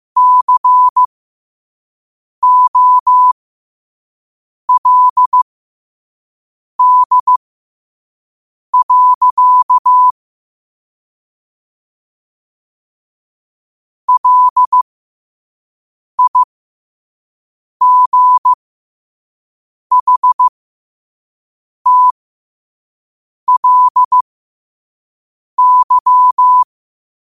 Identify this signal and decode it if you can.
Morse code, heard as a keyed tone, about 15 wpm, spaced as for 5 wpm, 1000 Hz. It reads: COLD. LIGHTLY
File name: qotdfiles / 5wpm